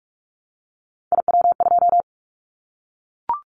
Morse code, heard as a keyed tone, about 44 wpm, 700 Hz. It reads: IW2
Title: Callsign Prefixes for Top 25 DXCC - Repeated 3x - 45wpm